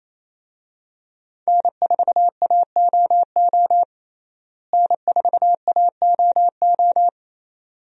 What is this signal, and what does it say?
Morse code, heard as a keyed tone, about 28 wpm, 700 Hz. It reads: N4AOO N4AOO
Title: USA Call Signs - 5 Characters - Repeated 2x - 28wpm